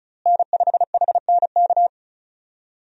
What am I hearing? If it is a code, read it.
Morse code, heard as a keyed tone, about 34 wpm, 700 Hz. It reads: N5HNK
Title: USA Call Signs - 5 Characters - Repeated 2x - 35wpm